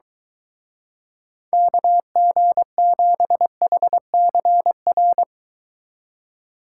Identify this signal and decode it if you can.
Morse code, heard as a keyed tone, about 23 wpm, 700 Hz. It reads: KG7HCR